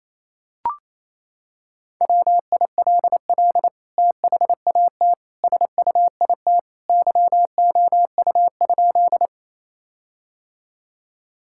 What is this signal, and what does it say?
Morse code, heard as a keyed tone, about 28 wpm, 700 Hz. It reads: WILL THAT SUIT YOU?
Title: Sentences from Top 1600 Words 28wpm